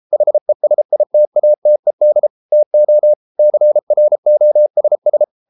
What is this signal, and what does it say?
Morse code, heard as a keyed tone, about 33 wpm, 600 Hz. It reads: HESITATED TO CROSS